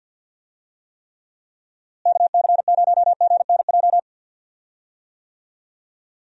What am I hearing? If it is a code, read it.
Morse code, heard as a keyed tone, about 48 wpm, 700 Hz. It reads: KC0GNJ
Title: USA Call Signs - 6 Characters - Repeated 3x - 50wpm